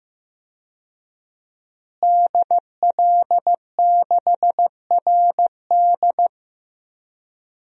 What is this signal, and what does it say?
Morse code, heard as a keyed tone, about 15 wpm, 700 Hz. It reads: DL6RD
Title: DXCC-100 Call Signs - 5 Characters - Repeated 2x - 15wpm